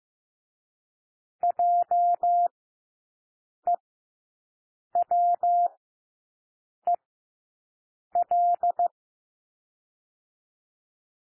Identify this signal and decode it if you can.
Morse code, heard as a keyed tone, about 15 wpm, spaced as for 6 wpm, 700 Hz. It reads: JEWEL
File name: cwmp3 / W1 SOUND WORDS 15 wpm